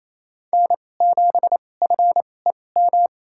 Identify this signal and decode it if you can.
Morse code, heard as a keyed tone, about 28 wpm, spaced as for 20 wpm, 700 Hz. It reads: N7FEM
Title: USA Call Signs - 5 Characters - 2x Character Spacing - 28wpm